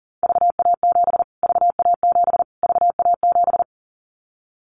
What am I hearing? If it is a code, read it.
Morse code, heard as a keyed tone, about 40 wpm, 700 Hz. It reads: VA7 VA7 VA7
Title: Callsign Prefixes for Top 25 DXCC - Repeated 3x - 40wpm